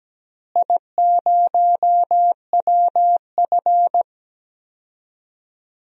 Morse code, heard as a keyed tone, about 17 wpm, 700 Hz. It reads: I0WF
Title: DXCC-100 Call Signs - 4 Characters - Repeated 2x - 17wpm